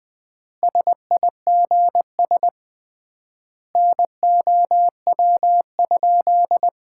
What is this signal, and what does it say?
Morse code, heard as a keyed tone, about 20 wpm, 700 Hz. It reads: SIGS NOW?
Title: Common QSO Phrases - 3x Word-Spacing - 20wpm